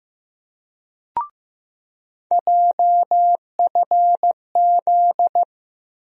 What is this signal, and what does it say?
Morse code, heard as a keyed tone, about 15 wpm, 700 Hz. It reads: JFZ
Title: USA Call Sign Postfixes - 3 Letters 15wpm